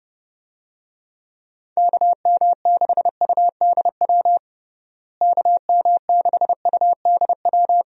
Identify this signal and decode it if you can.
Morse code, heard as a keyed tone, about 30 wpm, 700 Hz. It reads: KM6UDW KM6UDW
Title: USA Call Signs - 6 Characters - Repeated 3x - 30wpm